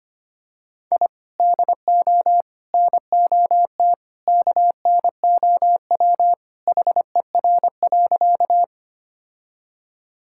Morse code, heard as a keyed tone, about 25 wpm, 700 Hz. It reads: I DO NOT KNOW HER.
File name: practice / 25wpm / Sentences with 2-Word Phrases - 25wpm